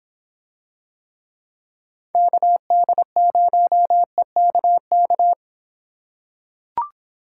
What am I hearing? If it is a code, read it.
Morse code, heard as a keyed tone, about 26 wpm, 700 Hz. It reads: KD0EKK